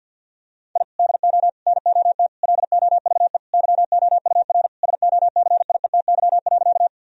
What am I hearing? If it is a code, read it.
Morse code, heard as a keyed tone, about 50 wpm, 700 Hz. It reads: I DO NOT LOVE YOUR SOCIETY.
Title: Sentences from Top 1000 Words - Review Part 4 - 50wpm